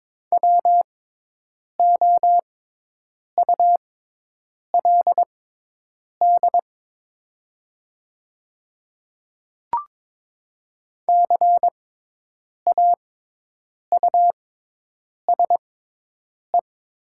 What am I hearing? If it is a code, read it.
Morse code, heard as a keyed tone, about 22 wpm, spaced as for 8 wpm, 700 Hz. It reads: WOULD CAUSE
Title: 6x Character Spacing - Top 200 Words - 22wpm